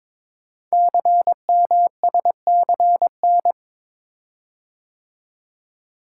Morse code, heard as a keyed tone, about 23 wpm, 690 Hz. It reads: CMSCN
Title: Code Groups - 5 Letters - 22wpm